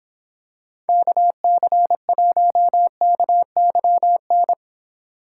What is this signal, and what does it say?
Morse code, heard as a keyed tone, about 26 wpm, 700 Hz. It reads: KC1KYN